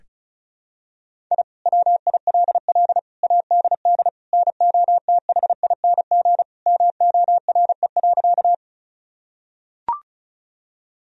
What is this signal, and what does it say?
Morse code, heard as a keyed tone, about 36 wpm, 700 Hz. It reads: I WILL ADD NOTHING MORE.
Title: Sentences from Top 1000 Words - Review Part 1 - 35wpm